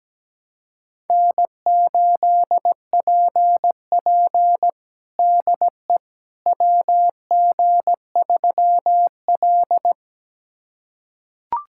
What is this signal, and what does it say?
Morse code, heard as a keyed tone, about 17 wpm, 700 Hz. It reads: N8PP DE WG3L
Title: USA Callsign DE Callsign - 4 Characters - 17wpm